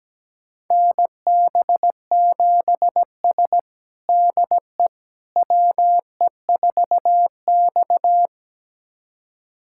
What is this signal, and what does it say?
Morse code, heard as a keyed tone, about 17 wpm, 700 Hz. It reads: NB7S DE WE4X
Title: USA Callsign DE Callsign - 4 Characters - 17wpm